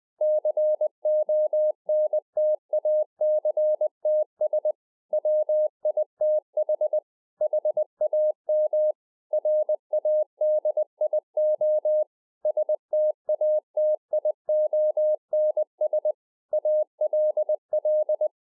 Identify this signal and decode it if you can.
Morse code, heard as a keyed tone, about 20 wpm, 610 Hz. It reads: CONTACTS WITH HAM RADIO STATIONS ALL
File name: 2011-04-mp3 / A SMALL TRANCEIVER 3 20WPM